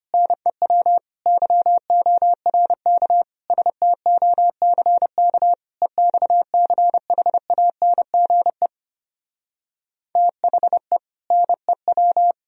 Morse code, heard as a keyed tone, about 29 wpm, 700 Hz. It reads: NEW YORK STOCK EXCHANGE THE NEW
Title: Speed-Racing - Sets of 5 Words 25wpm